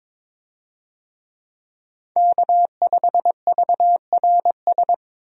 Morse code, heard as a keyed tone, about 22 wpm, 700 Hz. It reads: K5VRS
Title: USA Call Signs - 5 Characters - 22wpm